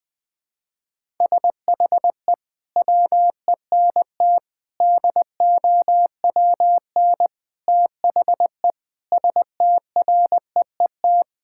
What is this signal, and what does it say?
Morse code, heard as a keyed tone, about 19 wpm, 700 Hz. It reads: SHE WENT DOWN THE STREET
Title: Sentences from Top 1000 Words - Review Part 5 - 20wpm